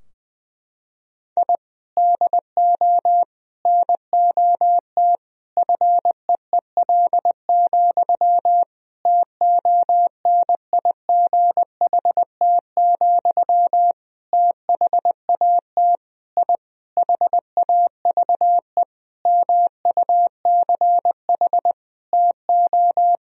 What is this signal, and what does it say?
Morse code, heard as a keyed tone, about 20 wpm, 700 Hz. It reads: I DO NOT FEEL, TONIGHT, THAT I HAVE MUCH TO
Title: Sentences from Top 1000 Words - Review Part 3 - 20wpm